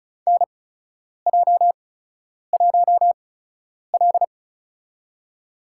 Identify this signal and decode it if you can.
Morse code, heard as a keyed tone, about 36 wpm, spaced as for 10 wpm, 700 Hz. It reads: NJ1L
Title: USA Call Signs - 8x Character Spacing - 35wpm